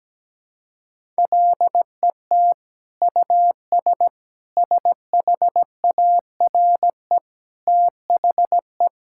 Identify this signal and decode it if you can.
Morse code, heard as a keyed tone, about 17 wpm, 700 Hz. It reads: LET US SHARE THE
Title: Sentences from Top 1000 Words - Review Part 6 - 17wpm